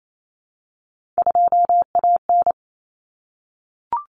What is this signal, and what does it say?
Morse code, heard as a keyed tone, about 28 wpm, 700 Hz. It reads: 2AN
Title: Common Cut Numbers - 3-Digit 28wpm